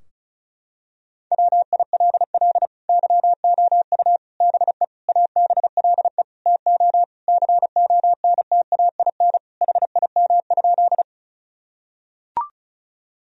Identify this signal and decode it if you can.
Morse code, heard as a keyed tone, about 36 wpm, 700 Hz. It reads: WILL YOU BE ABLE TO CONTAIN HIM?
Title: Sentences from Top 1000 Words - Review Part 2 - 35wpm